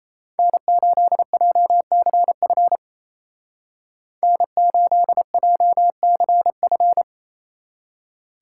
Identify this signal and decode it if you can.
Morse code, heard as a keyed tone, about 34 wpm, 700 Hz. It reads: N8JCF N8JCF
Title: Speed Racing - USA 1x3 Call Signs 28wpm